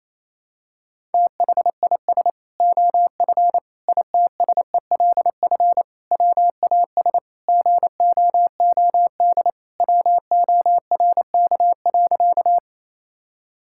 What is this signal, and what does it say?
Morse code, heard as a keyed tone, about 28 wpm, 700 Hz. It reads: THIS OF ITSELF WAS GOOD WORK.
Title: Sentences from Top 1000 Words - Review Part 5 - 28wpm